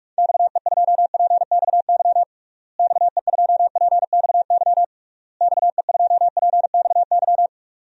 Morse code, heard as a keyed tone, about 46 wpm, 700 Hz. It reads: XE2PXY XE2PXY XE2PXY
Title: DXCC-100 Call Signs - 6 Characters - Repeated 3x - 45wpm